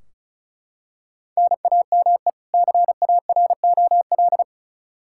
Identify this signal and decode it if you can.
Morse code, heard as a keyed tone, about 34 wpm, 700 Hz. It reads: NAME CAROL
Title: K1USN SST QSO Practice 35wpm